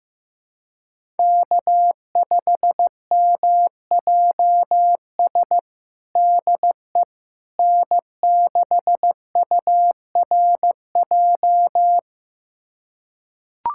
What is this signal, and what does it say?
Morse code, heard as a keyed tone, about 15 wpm, 700 Hz. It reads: K5MJS DE N6URJ
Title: USA Callsign DE Callsign - 5 Characters - 15wpm